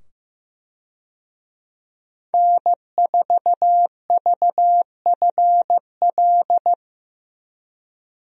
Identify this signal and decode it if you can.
Morse code, heard as a keyed tone, about 15 wpm, 700 Hz. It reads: N4VFL
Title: USA Call Signs - 5 Characters - 15wpm